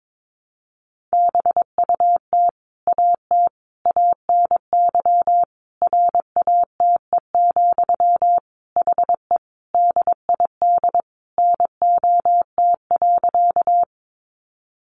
Morse code, heard as a keyed tone, about 22 wpm, 700 Hz. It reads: BUT AT ANY RATE, HE DID NOT.
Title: Sentences from Top 700 Words 22wpm